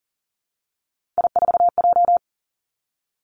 Morse code, heard as a keyed tone, about 40 wpm, 700 Hz. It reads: I4J
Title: Three Challenging Letter-Number Combinations 40wpm